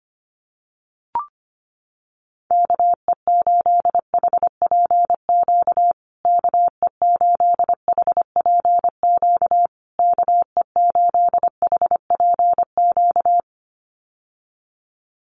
Morse code, heard as a keyed tone, about 25 wpm, 700 Hz. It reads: KE8HPQ KE8HPQ KE8HPQ
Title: USA Call Signs Repeated 3x 25wpm